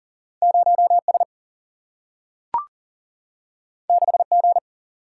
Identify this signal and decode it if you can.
Morse code, heard as a keyed tone, about 40 wpm, 700 Hz. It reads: 0S 6G
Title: Single-Letter Single-Number 40wpm